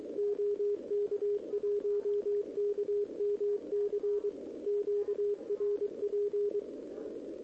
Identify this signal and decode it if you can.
Morse code, heard as a keyed tone, about 23 wpm, 410 Hz, with radio noise. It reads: OK1KMC QRP
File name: OK QRP 2026 / OKQRP2026